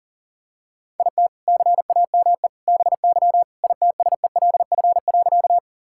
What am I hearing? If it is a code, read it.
Morse code, heard as a keyed tone, about 37 wpm, 700 Hz. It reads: IT CAME BY ITSELF.
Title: Sentences from Top 1000 Words - Review Part 5 - 40wpm